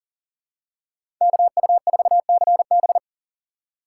Morse code, heard as a keyed tone, about 40 wpm, 700 Hz. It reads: KU4CB
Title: USA Call Signs - 5 Characters - 3x Character Spacing - 40wpm